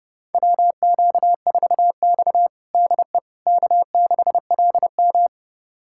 Morse code, heard as a keyed tone, about 30 wpm, 700 Hz. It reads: WQ4X DE K6LM
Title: USA Callsign DE Callsign - 4 Characters - 30wpm